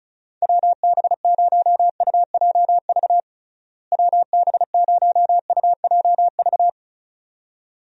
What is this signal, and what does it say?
Morse code, heard as a keyed tone, about 36 wpm, 700 Hz. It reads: WB0UJV WB0UJV
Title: USA Call Signs - 6 Characters - Repeated 2x - 35wpm